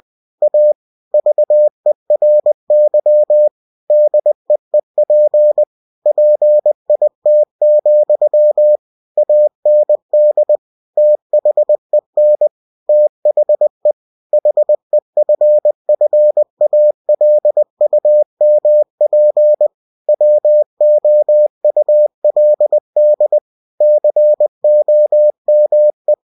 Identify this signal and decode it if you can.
Morse code, heard as a keyed tone, about 20 wpm, 600 Hz. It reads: A VERY DEEP PIT, AND THEN THE HEFFALUMP WOULD COME
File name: winnie-20wpm